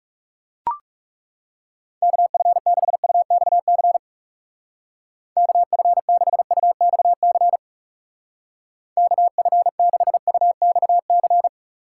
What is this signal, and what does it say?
Morse code, heard as a keyed tone, about 36 wpm, 700 Hz. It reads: KF6UXC KF6UXC KF6UXC
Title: Speed Racing - USA Call Signs 30wpm